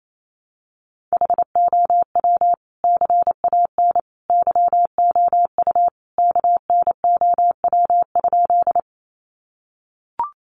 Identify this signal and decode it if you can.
Morse code, heard as a keyed tone, about 28 wpm, 700 Hz. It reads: HOW CAN YOU KNOW?
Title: Sentences from Top 100 Words 28wpm